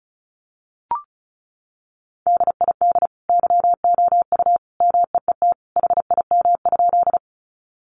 Morse code, heard as a keyed tone, about 34 wpm, 700 Hz. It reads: DID YOU MEET HIM?
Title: Sentences from Top 1100 Words 35wpm